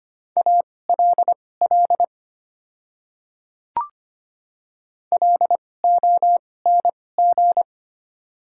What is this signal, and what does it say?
Morse code, heard as a keyed tone, about 25 wpm, spaced as for 18 wpm, 700 Hz. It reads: ALL LONG
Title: 2x Character Spacing - Top 100 Words - 25wpm